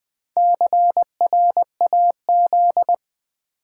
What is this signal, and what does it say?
Morse code, heard as a keyed tone, about 20 wpm, 700 Hz. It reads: CRAZ
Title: Build A Word - 1 Letter At A Time - 5-Letter Words - 20wpm